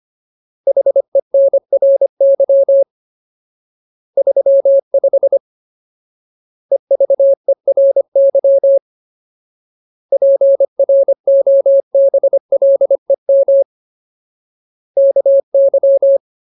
Morse code, heard as a keyed tone, about 25 wpm, 550 Hz. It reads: HENRY 35 EVERY PROBLEM KY
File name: Ultimate Word List / 25WPM Ultimate